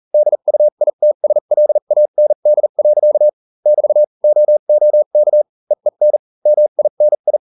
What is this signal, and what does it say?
Morse code, heard as a keyed tone, about 40 wpm, spaced as for 35 wpm, 600 Hz. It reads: DUITSLAND. <BT> OOK EEN MINI